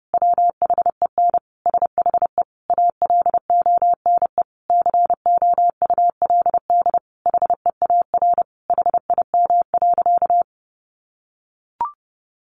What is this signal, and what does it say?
Morse code, heard as a keyed tone, about 30 wpm, 700 Hz. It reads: WHEN SHE ALONE COULD HEAR HIM.
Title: Sentences from Top 1000 Words 30wpm